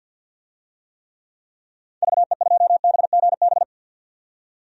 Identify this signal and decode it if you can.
Morse code, heard as a keyed tone, about 51 wpm, 700 Hz. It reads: VE2BGB
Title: DXCC-100 Call Signs - 6 Characters - 50wpm